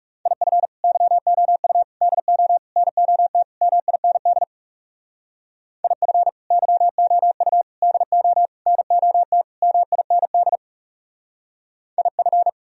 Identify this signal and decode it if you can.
Morse code, heard as a keyed tone, about 36 wpm, 700 Hz. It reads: IF YOU DO NOT MIND IF YOU DO NOT MIND IF
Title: Speed-Racing - Sets of 5 Words 30wpm